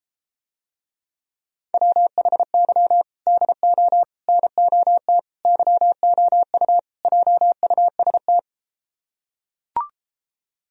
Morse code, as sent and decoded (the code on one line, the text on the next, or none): .-- .... -.-- / -.. --- / -. --- - / -.-- --- ..- / .--- ..- ... -
WHY DO NOT YOU JUST